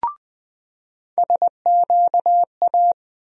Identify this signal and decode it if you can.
Morse code, heard as a keyed tone, about 20 wpm, 700 Hz. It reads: SQA
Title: USA Call Sign Postfixes - 3 Letters 20wpm